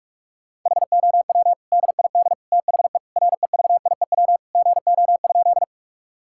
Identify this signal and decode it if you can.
Morse code, heard as a keyed tone, about 46 wpm, 700 Hz. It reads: HOW DID THE REVIEW GO?
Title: Sentences from Top 1000 Words - Review Part 6 - 45wpm